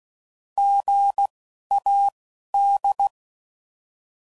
Morse code, heard as a keyed tone, about 16 wpm, spaced as for 11 wpm, 800 Hz. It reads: GAD
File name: Prefixes / By First Letter